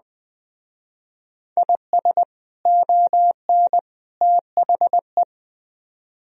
Morse code, heard as a keyed tone, about 19 wpm, 700 Hz. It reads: IS ON THE